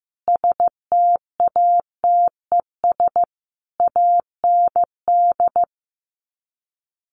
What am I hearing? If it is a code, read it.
Morse code, heard as a keyed tone, about 15 wpm, 700 Hz. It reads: STATES AND
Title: Sets of 3 Words 15wpm